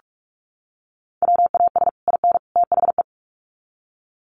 Morse code, heard as a keyed tone, about 44 wpm, 700 Hz. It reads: WAS IN THE
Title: Sets of 3 Words 45wpm